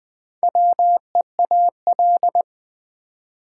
Morse code, heard as a keyed tone, about 20 wpm, 700 Hz. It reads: WEAL
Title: Build A Word - 2 Letters At A Time - 6-Letter Words - 20wpm